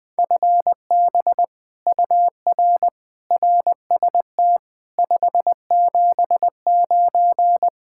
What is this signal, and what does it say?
Morse code, heard as a keyed tone, about 21 wpm, 700 Hz. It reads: FB UR RST 579
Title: Speed-Racing Common QSO Phrases - 20wpm